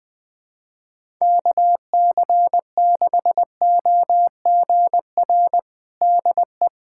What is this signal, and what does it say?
Morse code, heard as a keyed tone, about 21 wpm, 700 Hz. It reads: KC6OGR DE
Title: USA Callsign DE Callsign 20wpm